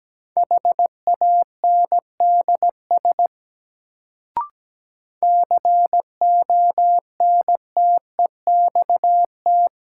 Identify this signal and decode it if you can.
Morse code, heard as a keyed tone, about 17 wpm, 700 Hz. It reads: HANDS CONTEXT
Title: Sets of 4 Words - New Word Drill - 17wpm